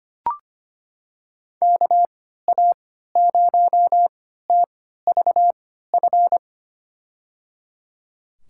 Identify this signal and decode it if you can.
Morse code, heard as a keyed tone, about 25 wpm, spaced as for 14 wpm, 700 Hz. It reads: KA0TVF
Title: USA Call Signs - 3x Character Spacing - 25wpm